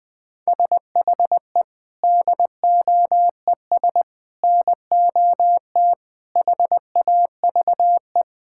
Morse code, heard as a keyed tone, about 20 wpm, 700 Hz. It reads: SHE DOES NOT HAVE
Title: Sentences with 3-Word Phrases - Spoken Beginning - 20wpm